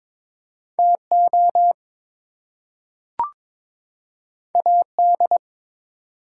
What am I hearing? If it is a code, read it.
Morse code, heard as a keyed tone, about 22 wpm, 700 Hz. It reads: TO AD
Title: Two Letter Words - 22wpm